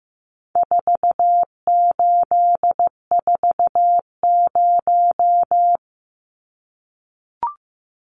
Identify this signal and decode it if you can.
Morse code, heard as a keyed tone, about 15 wpm, 700 Hz. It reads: 4840
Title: Numbers - 4-Digit 15wpm